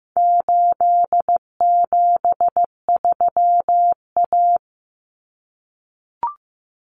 Morse code, heard as a keyed tone, about 15 wpm, 700 Hz. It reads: 873A
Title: Common Cut Numbers - 4-Digit 15wpm